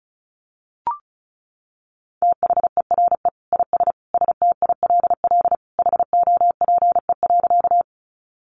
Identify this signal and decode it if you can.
Morse code, heard as a keyed tone, about 34 wpm, 700 Hz. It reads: THERE IS STILL HOPE.
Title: Sentences from Top 1000 Words 35wpm